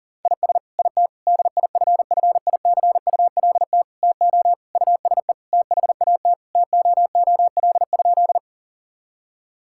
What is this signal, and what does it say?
Morse code, heard as a keyed tone, about 40 wpm, 700 Hz. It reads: IS IT DIFFICULT TO USE THAT TOOL?
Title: Sentences from Top 1000 Words - Review Part 6 - 40wpm